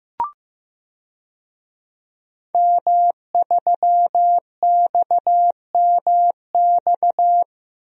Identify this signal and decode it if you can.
Morse code, heard as a keyed tone, about 15 wpm, 700 Hz. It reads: M3XMX
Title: DXCC-100 Call Signs - 5 Characters - 15wpm